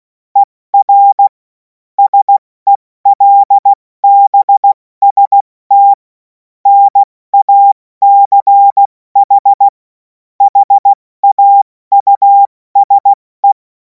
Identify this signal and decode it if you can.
Morse code, heard as a keyed tone, about 16 wpm, spaced as for 14 wpm, 800 Hz. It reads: ER SELBST NACH HAUSE
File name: Morsekurs / Witze / Witz 31. März 2026 70 F80